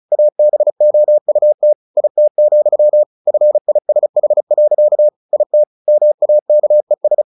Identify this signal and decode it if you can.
Morse code, heard as a keyed tone, about 34 wpm, 600 Hz. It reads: ABOUT IT, FISH. IT MAKES
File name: the old man and the sea 2 35wpm